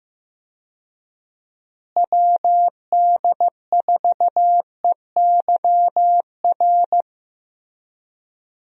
Morse code, heard as a keyed tone, about 15 wpm, 700 Hz. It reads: WD4EYR
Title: USA Call Signs - 6 Characters - Repeated 3x - 15wpm